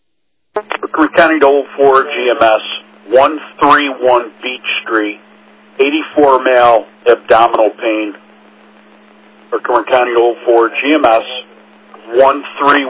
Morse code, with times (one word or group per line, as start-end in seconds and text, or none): none